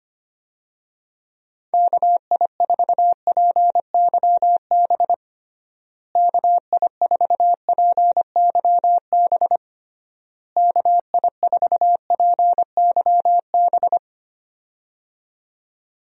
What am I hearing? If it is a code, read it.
Morse code, heard as a keyed tone, about 25 wpm, 700 Hz. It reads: KI4PYB KI4PYB KI4PYB
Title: USA Call Signs - 6 Characters - Repeated 3x - 25wpm